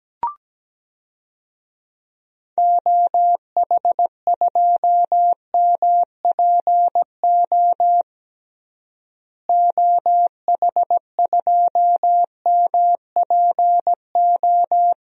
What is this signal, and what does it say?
Morse code, heard as a keyed tone, about 17 wpm, 700 Hz. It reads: OH2MPO OH2MPO
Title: DXCC-100 Call Signs - 6 Characters - Repeated 3x - 17wpm